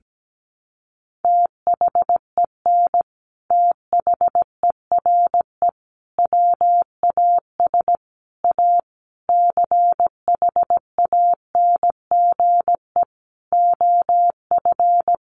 Morse code, heard as a keyed tone, about 17 wpm, 700 Hz. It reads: THEN THERE WAS A CHANGE OF
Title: Sentences from Top 1100 Words 17wpm